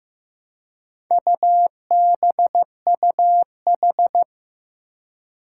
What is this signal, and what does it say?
Morse code, heard as a keyed tone, about 15 wpm, 700 Hz. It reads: UBUH
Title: Four Challenging Letter-Number Combinations 15wpm